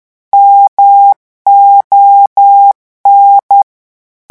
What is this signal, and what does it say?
Morse code, heard as a keyed tone, about 10 wpm, 800 Hz. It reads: MON
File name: morse mp3 / revision francais 10 mots minute